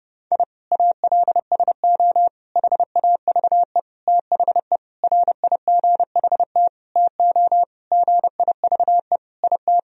Morse code, heard as a keyed tone, about 30 wpm, 700 Hz. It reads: I ALSO HAVE THE RIGHT TO GIVE IT
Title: Sentences from Top 1000 Words - Review Part 1 - 30wpm